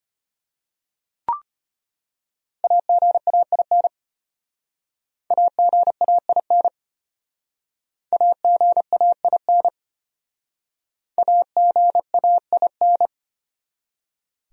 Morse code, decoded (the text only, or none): AGAIN AGAIN AGAIN AGAIN